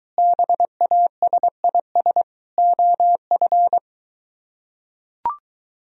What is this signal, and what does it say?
Morse code, heard as a keyed tone, about 23 wpm, 700 Hz. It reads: BASIS OF